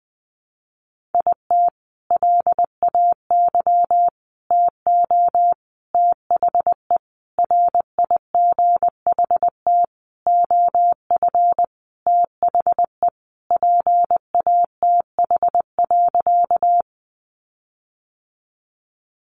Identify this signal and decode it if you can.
Morse code, heard as a keyed tone, about 20 wpm, 700 Hz. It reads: IT LAY TO THE RIGHT OF THE PATH.
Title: Sentences from Top 1400 Words 20wpm